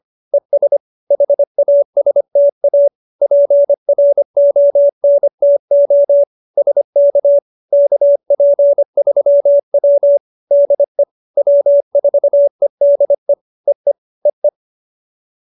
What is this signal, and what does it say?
Morse code, heard as a keyed tone, about 25 wpm, 570 Hz. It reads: ES HASTA PRONTO SK KP3W DE W4EDE EE EE